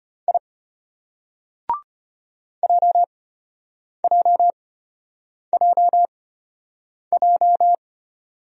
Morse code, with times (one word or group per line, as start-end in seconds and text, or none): 0.3-0.4 I
2.6-7.8 JJJJ